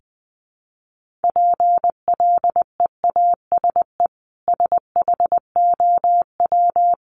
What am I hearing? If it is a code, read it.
Morse code, heard as a keyed tone, about 20 wpm, 700 Hz. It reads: PLEASE SHOW
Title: Sentences from Top 1500 Words 20wpm